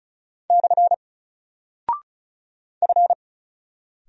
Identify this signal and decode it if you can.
Morse code, heard as a keyed tone, about 36 wpm, 700 Hz. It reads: / F
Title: Single Character (Letter:Number:Punctuation:Prosign) Alt 35wpm